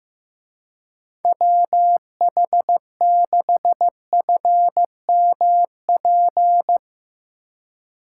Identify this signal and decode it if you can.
Morse code, heard as a keyed tone, about 15 wpm, 700 Hz. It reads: WH6FMP
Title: Speed Racing - USA 2x3 Call Signs 15wpm